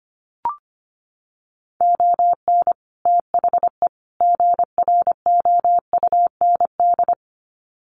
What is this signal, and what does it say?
Morse code, heard as a keyed tone, about 25 wpm, 700 Hz. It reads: ON THE GROUND
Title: Sets of 3 Words 25wpm